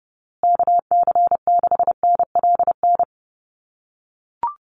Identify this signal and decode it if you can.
Morse code, heard as a keyed tone, about 30 wpm, 700 Hz. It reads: KC6NLN
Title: USA Call Signs Repeated 3x 30wpm